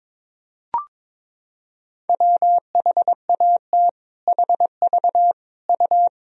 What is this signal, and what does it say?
Morse code, heard as a keyed tone, about 21 wpm, 700 Hz. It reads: WHAT HV U
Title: Sentences from Top 500 Words and Abbreviations 22wpm